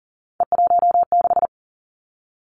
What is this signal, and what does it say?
Morse code, heard as a keyed tone, about 40 wpm, 700 Hz. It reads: E16
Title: Three Challenging Letter-Number Combinations 40wpm